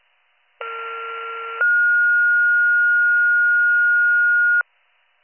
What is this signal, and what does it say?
Two-tone page, heard as a one-second tone, then a three-second tone, about 1370 then 1430 Hz. It is heard from 0.6 to 4.6 s.